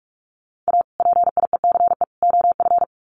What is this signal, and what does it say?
Morse code, heard as a keyed tone, about 44 wpm, 700 Hz. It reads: A PIECE OF
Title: Sets of 3 Words 45wpm